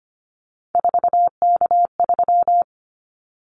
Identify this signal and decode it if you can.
Morse code, heard as a keyed tone, about 25 wpm, 700 Hz. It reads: 4K3
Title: Callsign Prefixes for Top 100 DXCC - 25wpm